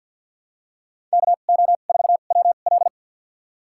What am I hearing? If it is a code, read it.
Morse code, heard as a keyed tone, about 49 wpm, spaced as for 35 wpm, 700 Hz. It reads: KO4WL
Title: USA Call Signs - 5 Characters - 2x Character Spacing - 50wpm